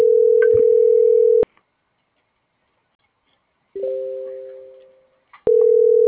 Ringback tone (North America), 440 + 480 Hz.
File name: CENTRAL GENERAL / out laura.albarran